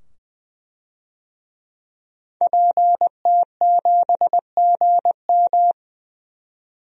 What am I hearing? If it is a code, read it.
Morse code, heard as a keyed tone, about 20 wpm, 700 Hz. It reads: PT7GM
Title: DXCC-100 Call Signs - 5 Characters - 20wpm